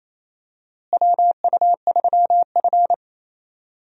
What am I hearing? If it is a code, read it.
Morse code, heard as a keyed tone, about 28 wpm, 700 Hz. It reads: WU3F
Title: USA Call Signs - 4 Characters - Repeated 3x - 28wpm